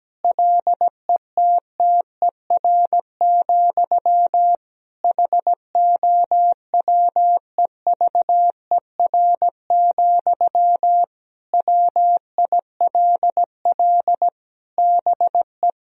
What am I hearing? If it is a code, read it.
Morse code, heard as a keyed tone, about 17 wpm, 700 Hz. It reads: LETTER, HOWEVER, WILL BE
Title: Sentences from Top 1000 Words - Review Part 4 - 17wpm